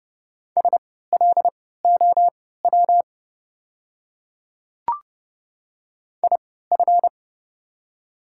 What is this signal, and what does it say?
Morse code, heard as a keyed tone, about 30 wpm, spaced as for 17 wpm, 690 Hz. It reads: SLOW IF
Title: 3x Character Spacing - Top 500 Words - 30wpm